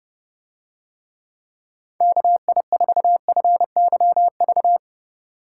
Morse code, heard as a keyed tone, about 29 wpm, 700 Hz. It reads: KI4FYV